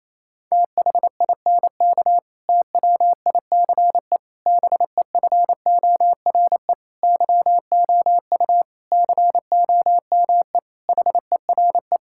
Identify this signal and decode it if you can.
Morse code, heard as a keyed tone, about 28 wpm, 700 Hz. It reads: THINK TWICE BEFORE YOU COME HERE
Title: Sentences from Top 1600 Words 28wpm